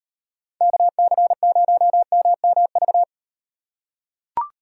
Morse code, heard as a keyed tone, about 38 wpm, 700 Hz. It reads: KC0MMV